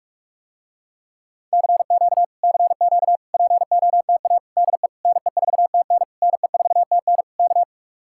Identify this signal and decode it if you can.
Morse code, heard as a keyed tone, about 45 wpm, 700 Hz. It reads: CQ CQ POTA DE NE4TN NE4TN K